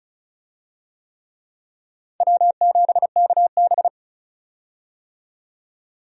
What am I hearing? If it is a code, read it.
Morse code, heard as a keyed tone, about 34 wpm, 700 Hz. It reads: W7KB
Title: USA Call Signs - 4 Characters - 35wpm